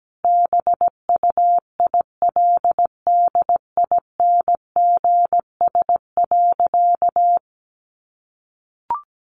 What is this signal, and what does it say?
Morse code, heard as a keyed tone, about 17 wpm, 700 Hz. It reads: BUILDINGS.
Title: Easy Phrases 17wpm